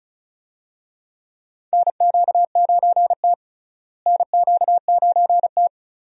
Morse code, heard as a keyed tone, about 36 wpm, 700 Hz. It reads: NQ9T NQ9T
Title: USA Call Signs - 4 Characters - Repeated 2x - 35wpm